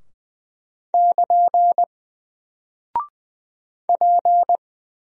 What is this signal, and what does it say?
Morse code, heard as a keyed tone, about 20 wpm, 700 Hz. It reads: <KN> P
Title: Single Character (Letter:Number:Punctuation:Prosign) 20wpm